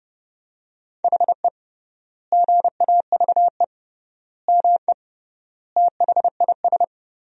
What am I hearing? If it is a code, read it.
Morse code, heard as a keyed tone, about 30 wpm, 700 Hz. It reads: HE GAVE ME THIS
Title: 3x Word-Spacing - Sentences from Top 500 Words - 30wpm